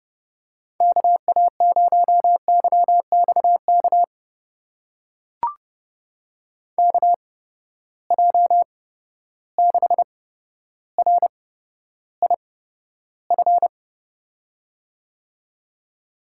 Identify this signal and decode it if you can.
Morse code, heard as a keyed tone, about 30 wpm, spaced as for 13 wpm, 700 Hz. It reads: KA0YXK KJ6RIF
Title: USA Call Signs - 6 Characters - 8x Character Spacing - 30wpm